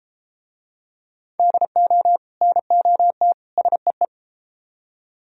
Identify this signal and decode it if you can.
Morse code, heard as a keyed tone, about 33 wpm, 700 Hz. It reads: DO NOT SEE